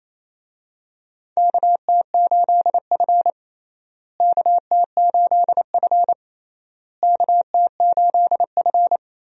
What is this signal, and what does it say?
Morse code, heard as a keyed tone, about 28 wpm, 700 Hz. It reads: KT8F KT8F KT8F
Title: USA Call Signs - 4 Characters - Repeated 3x - 28wpm